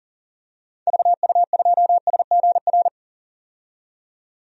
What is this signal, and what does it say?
Morse code, heard as a keyed tone, about 40 wpm, 700 Hz. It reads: VU2SGR
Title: DXCC-100 Call Signs - 6 Characters - 40wpm